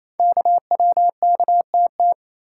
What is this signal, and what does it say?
Morse code, heard as a keyed tone, about 27 wpm, 700 Hz. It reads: KWKTT
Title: Code Groups - 5 Letters - 28wpm